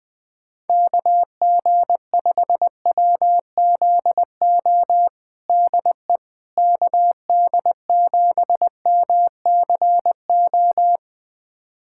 Morse code, heard as a keyed tone, about 20 wpm, 700 Hz. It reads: KG5WZO DE KD7MCO
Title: USA Callsign DE Callsign 20wpm